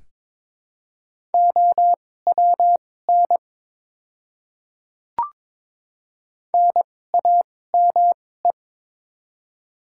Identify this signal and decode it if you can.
Morse code, heard as a keyed tone, about 22 wpm, spaced as for 16 wpm, 700 Hz. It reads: OWN NAME
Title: 2x Character Spacing - Top 500 Words - 22wpm